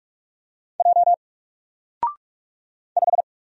Morse code, heard as a keyed tone, about 44 wpm, 700 Hz. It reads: J 5
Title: Single Character (Letter:Number:Punctuation:Prosign) Alt 45wpm